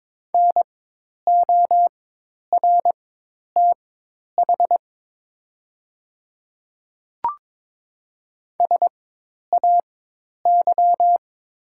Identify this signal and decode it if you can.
Morse code, heard as a keyed tone, about 22 wpm, spaced as for 10 wpm, 700 Hz. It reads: NORTH SAY
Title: 4x Character Spacing - Top 300 Words - 22wpm